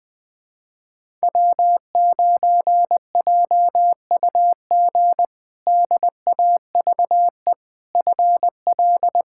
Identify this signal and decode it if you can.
Morse code, heard as a keyed tone, about 20 wpm, 700 Hz. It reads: W9JUG DAVE FL
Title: K1USN SST QSO Practice - 20wpm